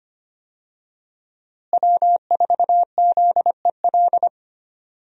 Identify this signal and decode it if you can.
Morse code, heard as a keyed tone, about 25 wpm, 700 Hz. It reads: W4ZEL
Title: USA Call Signs - 5 Characters - Repeated 2x - 25wpm